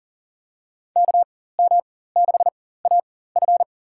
Morse code, heard as a keyed tone, about 40 wpm, spaced as for 19 wpm, 700 Hz. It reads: KM6AF
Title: USA Call Signs - 5 Characters - 4x Character Spacing - 40wpm